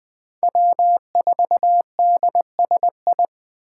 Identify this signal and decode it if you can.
Morse code, heard as a keyed tone, about 20 wpm, 700 Hz. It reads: W4DSI
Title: Speed Racing - USA Call Signs 17wpm